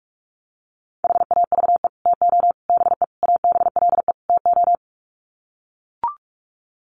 Morse code, heard as a keyed tone, about 44 wpm, 700 Hz. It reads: HAVE TO BE ABLE TO
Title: Sets of 5 Words 45wpm